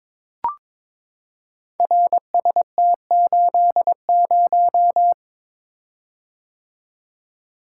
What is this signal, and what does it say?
Morse code, heard as a keyed tone, about 22 wpm, 700 Hz. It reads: RST80
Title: Code Groups - 5 Letter:Numbers - 22wpm